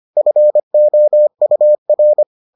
Morse code, heard as a keyed tone, about 25 wpm, 600 Hz. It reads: FOUR